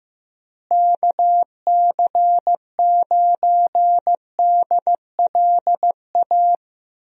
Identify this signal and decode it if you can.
Morse code, heard as a keyed tone, about 15 wpm, 700 Hz. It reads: KC9DLA
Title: USA Call Signs - 6 Characters - Repeated 3x - 15wpm